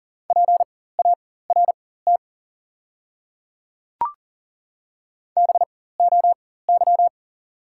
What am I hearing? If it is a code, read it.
Morse code, heard as a keyed tone, about 37 wpm, spaced as for 18 wpm, 690 Hz. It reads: PART BOY
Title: 4x Character Spacing - Top 500 Words - 40wpm